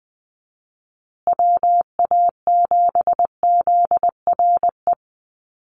Morse code, heard as a keyed tone, about 20 wpm, 700 Hz. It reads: WA7ZRE
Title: USA Call Signs Repeated 2x 20wpm